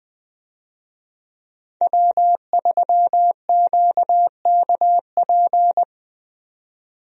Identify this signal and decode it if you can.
Morse code, heard as a keyed tone, about 20 wpm, 700 Hz. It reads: W3QKP
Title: USA Call Signs - 5 Characters - 20wpm